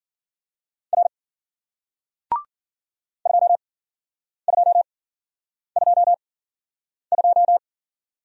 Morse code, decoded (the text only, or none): R 2222